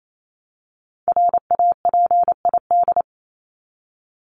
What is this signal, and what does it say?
Morse code, heard as a keyed tone, about 28 wpm, 700 Hz. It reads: RAPID
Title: LY Suffix - Top 100 Words with Base Words 28wpm